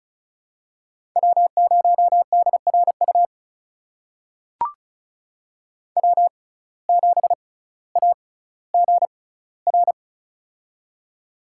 Morse code, heard as a keyed tone, about 36 wpm, spaced as for 18 wpm, 700 Hz. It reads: W0DRU W7AGR
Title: USA Call Signs - 5 Characters - 6x Character Spacing - 35wpm